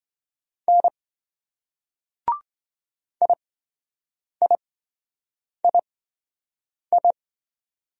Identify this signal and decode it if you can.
Morse code, heard as a keyed tone, about 26 wpm, spaced as for 7 wpm, 700 Hz. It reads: N IIII